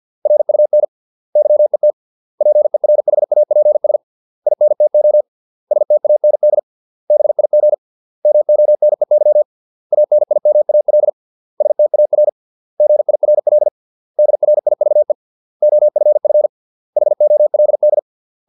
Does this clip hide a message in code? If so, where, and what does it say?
Yes: Morse code from 0.3 to 18.0 s, heard as a keyed tone, about 50 wpm, 600 Hz. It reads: RUN YET PERHAPS INTO STAND BIG MONEY ANIMAL STAR GIRL DRIVE OFF HOLD